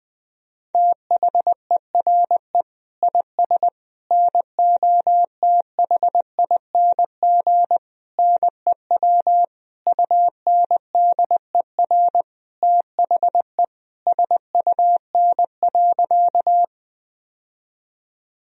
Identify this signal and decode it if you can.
Morse code, heard as a keyed tone, about 20 wpm, 700 Hz. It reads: THERE IS NOTHING NEW UNDER THE SUN.
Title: Sentences from Top 1000 Words - Review Part 1 - 20wpm